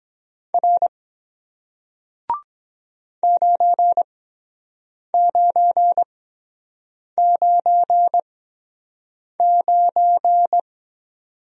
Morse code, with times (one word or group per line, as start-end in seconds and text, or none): 0.5-0.9 R
3.2-10.6 9999